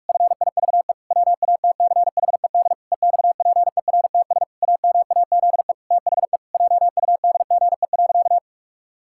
Morse code, heard as a keyed tone, about 46 wpm, 700 Hz. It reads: FIVE WATCHED EXPERTS AMAZE THE JUDGE.
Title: Pangram Challenge v2 - 45wpm